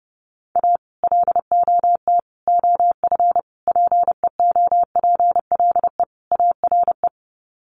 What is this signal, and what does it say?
Morse code, heard as a keyed tone, about 30 wpm, 700 Hz. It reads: A LOT OF PEOPLE ARE
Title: Sets of 5 Words 30wpm